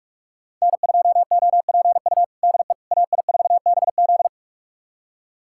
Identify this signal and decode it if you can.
Morse code, heard as a keyed tone, about 42 wpm, 690 Hz. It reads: N2OPU DE AI4BZ
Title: USA Callsign DE Callsign 45wpm